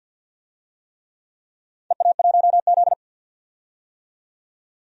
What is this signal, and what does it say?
Morse code, heard as a keyed tone, about 50 wpm, 700 Hz. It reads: EA1Z
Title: DXCC-100 Call Signs - 4 Characters - 50wpm